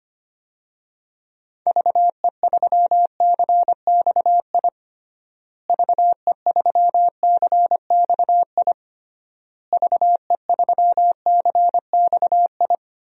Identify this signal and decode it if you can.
Morse code, heard as a keyed tone, about 25 wpm, 700 Hz. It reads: VE3CXI VE3CXI VE3CXI
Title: DXCC-100 Call Signs - 6 Characters - Repeated 3x - 25wpm